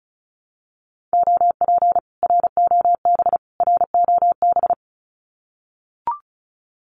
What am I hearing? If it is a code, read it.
Morse code, heard as a keyed tone, about 34 wpm, 700 Hz. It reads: OP ROB ROB
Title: Common QSO Phrases Encore 35wpm